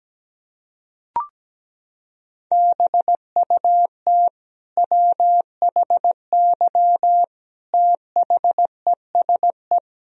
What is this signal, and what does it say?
Morse code, heard as a keyed tone, about 17 wpm, 700 Hz. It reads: BUT WHY THESE
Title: Sentences from Top 1000 Words - Review Part 1 - 17wpm